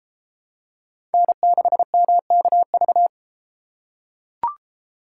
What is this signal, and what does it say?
Morse code, heard as a keyed tone, about 33 wpm, 700 Hz. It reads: N6MKV